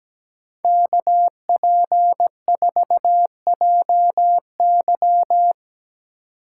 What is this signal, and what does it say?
Morse code, heard as a keyed tone, about 19 wpm, 700 Hz. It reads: KP4JY
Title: Speed Racing - USA Call Signs 17wpm